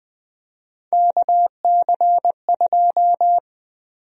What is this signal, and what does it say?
Morse code, heard as a keyed tone, about 20 wpm, 700 Hz. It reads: KC2
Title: USA Call Sign Prefixes - 20wpm